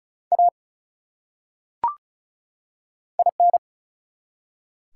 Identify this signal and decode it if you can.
Morse code, heard as a keyed tone, about 36 wpm, 700 Hz. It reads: A IN
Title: Top 180 QSO Elements 35wpm